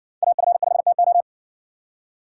Morse code, heard as a keyed tone, about 60 wpm, 700 Hz. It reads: AF5TO